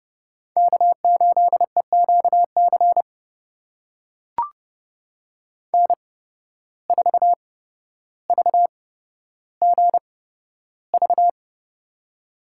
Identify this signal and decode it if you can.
Morse code, heard as a keyed tone, about 31 wpm, spaced as for 13 wpm, 700 Hz. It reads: K8EQC N4VGV
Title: USA Call Signs - 5 Characters - 8x Character Spacing - 30wpm